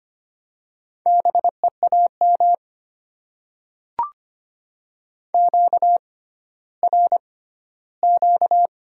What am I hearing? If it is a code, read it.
Morse code, heard as a keyed tone, about 25 wpm, spaced as for 14 wpm, 690 Hz. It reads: BEAM QRQ
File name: Top 90 QSO Elements - 6x Character Spacing - 25wpm